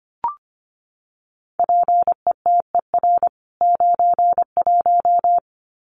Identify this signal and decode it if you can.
Morse code, heard as a keyed tone, about 25 wpm, 700 Hz. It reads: PETER 91
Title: CWT Exchange 25wpm